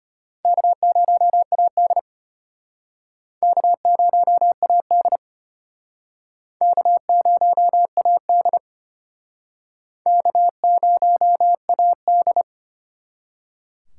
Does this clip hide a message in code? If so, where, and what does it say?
Yes: Morse code from 0.4 to 12.4 s, heard as a keyed tone, about 34 wpm, 700 Hz. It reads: K0AD K0AD K0AD K0AD